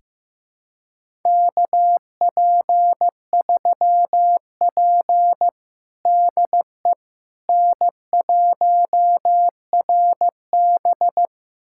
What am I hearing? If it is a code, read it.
Morse code, heard as a keyed tone, about 15 wpm, 700 Hz. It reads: KP3P DE N1RB
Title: USA Callsign DE Callsign - 4 Characters - 15wpm